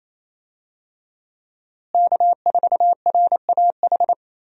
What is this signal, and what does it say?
Morse code, heard as a keyed tone, about 28 wpm, 700 Hz. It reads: K4RAH
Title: USA Call Signs - 5 Characters - Repeated 3x - 28wpm